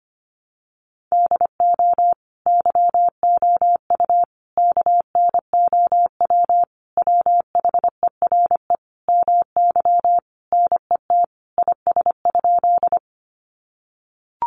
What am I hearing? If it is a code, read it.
Morse code, heard as a keyed tone, about 25 wpm, 700 Hz. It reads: DO YOU KNOW WHERE MY NET IS?
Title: Sentences from Top 1400 Words 25wpm